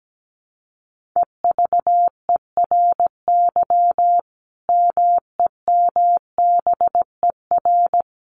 Morse code, heard as a keyed tone, about 17 wpm, 700 Hz. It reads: EVERY MEMBER
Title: Sentences from Top 1400 Words 17wpm